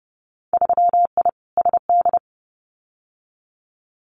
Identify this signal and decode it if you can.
Morse code, heard as a keyed tone, about 31 wpm, 700 Hz. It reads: 3I SD
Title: Winter Field Day Exchanges Encore 30wpm